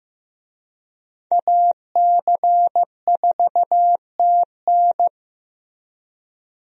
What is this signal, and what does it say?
Morse code, heard as a keyed tone, about 15 wpm, 700 Hz. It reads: AC4TN
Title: USA Call Signs - 5 Characters - Repeated 3x - 15wpm